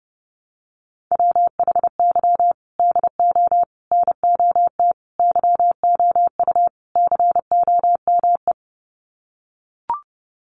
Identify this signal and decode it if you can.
Morse code, heard as a keyed tone, about 30 wpm, 700 Hz. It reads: WHY DO NOT YOU COME
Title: Sets of 5 Words Encore2 30wpm